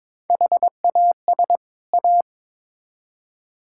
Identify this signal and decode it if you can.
Morse code, heard as a keyed tone, about 22 wpm, 700 Hz. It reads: HAS A
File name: Build A Phrase - 2-Word Phrases - 22wpm